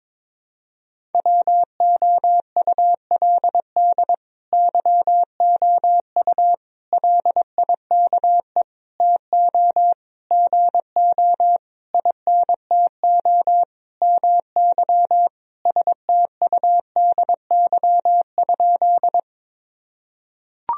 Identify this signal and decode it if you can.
Morse code, heard as a keyed tone, about 22 wpm, 700 Hz. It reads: WOULD YOU LIKE TO GO INTO MY STUDY?
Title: Sentences from Top 1000 Words - Review Part 1 - 22wpm